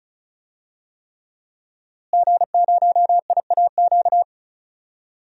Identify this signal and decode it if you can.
Morse code, heard as a keyed tone, about 34 wpm, 700 Hz. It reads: G0IAQ
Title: DXCC-100 Call Signs - 5 Characters - Repeated 3x - 35wpm